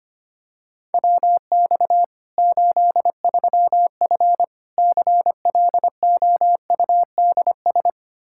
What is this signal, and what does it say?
Morse code, heard as a keyed tone, about 25 wpm, 700 Hz. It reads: WX 83F CLOUDS
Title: Practice QSO - Exchange 2 - Info - 25wpm